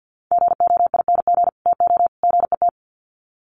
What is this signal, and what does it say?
Morse code, heard as a keyed tone, about 50 wpm, 700 Hz. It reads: GOING TO GET
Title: Sets of 3 Words 50wpm